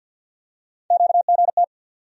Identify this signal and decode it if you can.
Morse code, heard as a keyed tone, about 46 wpm, 700 Hz. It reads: QGT
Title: USA Call Sign Postfixes - 3 Letters 50wpm